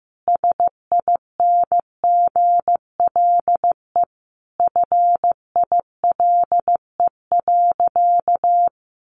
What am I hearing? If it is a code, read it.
Morse code, heard as a keyed tone, about 15 wpm, 700 Hz. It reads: SINGLE FILE.
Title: Sentences from Top 1200 Words 15wpm